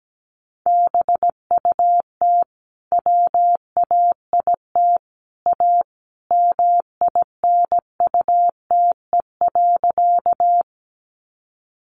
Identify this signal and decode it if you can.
Morse code, heard as a keyed tone, about 17 wpm, 700 Hz. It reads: BUT WAIT A MINUTE.
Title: Sentences from Top 500 Words 17wpm